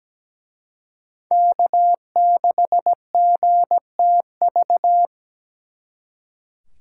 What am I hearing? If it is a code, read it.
Morse code, heard as a keyed tone, about 20 wpm, 700 Hz. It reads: K6GTV
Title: Speed Racing - USA 1x3 Call Signs 17wpm